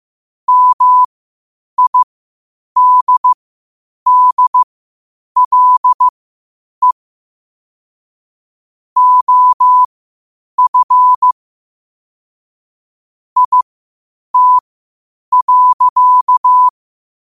Morse code, heard as a keyed tone, about 15 wpm, spaced as for 8 wpm, 1000 Hz. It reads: MIDDLE OF IT.
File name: qotdfiles / 8wpm